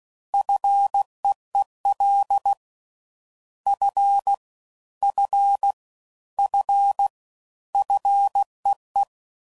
Morse code, heard as a keyed tone, about 16 wpm, 800 Hz. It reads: FEEL F F F FEE
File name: Prefixes / FLs & THs / Single Fs Ls